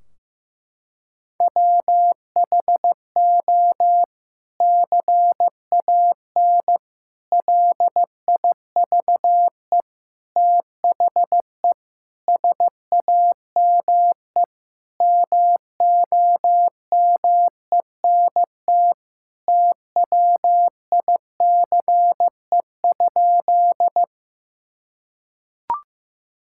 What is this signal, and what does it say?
Morse code, heard as a keyed tone, about 15 wpm, 700 Hz. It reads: WHO CAN LIVE THE SAME MOMENT TWICE?
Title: Sentences from Top 1600 Words 15wpm